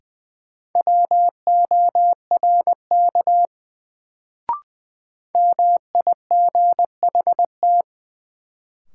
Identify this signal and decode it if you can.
Morse code, heard as a keyed tone, about 20 wpm, 700 Hz. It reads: WORK MIGHT
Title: Sets of 2 Words - New Word Drill - 20wpm